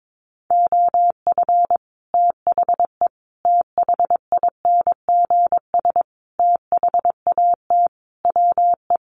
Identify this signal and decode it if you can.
Morse code, heard as a keyed tone, about 22 wpm, 700 Hz. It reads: OF THE THINGS THAT WE
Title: Sets of 5 Words Encore 22wpm